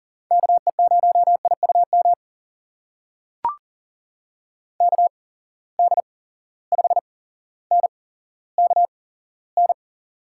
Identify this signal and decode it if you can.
Morse code, heard as a keyed tone, about 40 wpm, spaced as for 17 wpm, 700 Hz. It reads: KE0IUM KD5NKN
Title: USA Call Signs - 6 Characters - 8x Character Spacing - 40wpm